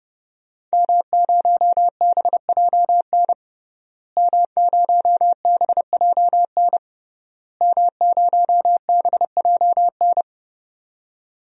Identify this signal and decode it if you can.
Morse code, heard as a keyed tone, about 30 wpm, 700 Hz. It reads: M0BJN M0BJN M0BJN
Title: DXCC-100 Call Signs - 5 Characters - Repeated 3x - 30wpm